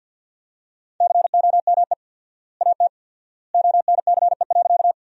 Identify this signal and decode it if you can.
Morse code, heard as a keyed tone, about 54 wpm, 700 Hz. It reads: COME AT ONCE.
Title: 4x Word-Spacing - Sentences from Top 300 Words - 50wpm